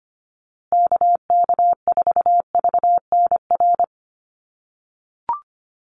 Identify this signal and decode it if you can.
Morse code, heard as a keyed tone, about 25 wpm, 700 Hz. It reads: KK4VNR
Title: USA Call Signs Repeated 2x 25wpm